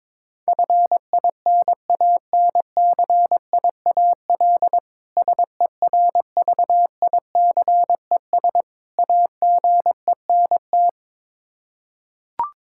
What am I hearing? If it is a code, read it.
Morse code, heard as a keyed tone, about 22 wpm, 700 Hz. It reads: FINANCIAL SERVICES AGENT
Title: Practice QSO - Exchange 3 - More Info - 22wpm